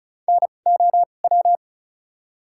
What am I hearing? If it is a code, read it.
Morse code, heard as a keyed tone, about 34 wpm, spaced as for 25 wpm, 700 Hz. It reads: NOW
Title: 2x Character Spacing - Top 300 Words - 35wpm